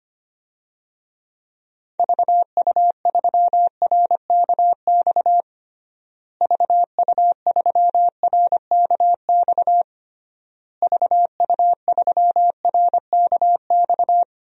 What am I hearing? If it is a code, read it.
Morse code, heard as a keyed tone, about 25 wpm, 700 Hz. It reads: VU3RKX VU3RKX VU3RKX
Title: DXCC-100 Call Signs - 6 Characters - Repeated 3x - 25wpm